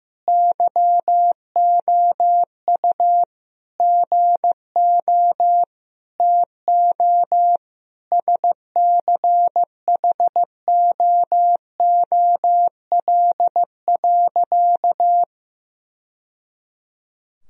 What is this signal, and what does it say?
Morse code, heard as a keyed tone, about 15 wpm, 700 Hz. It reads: YOU GO TO SCHOOL.
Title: Sentences from Top 1000 Words - Review Part 2 - 15wpm